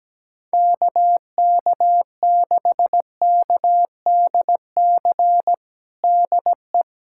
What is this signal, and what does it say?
Morse code, heard as a keyed tone, about 17 wpm, 700 Hz. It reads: KK6KDC DE
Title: USA Callsign DE Callsign 17wpm